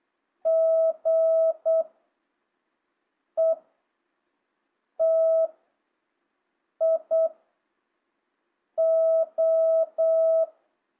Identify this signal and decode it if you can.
Morse code, heard as a keyed tone, about 8 wpm, 660 Hz. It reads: G E T I O
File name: jh8cbh / 2025cw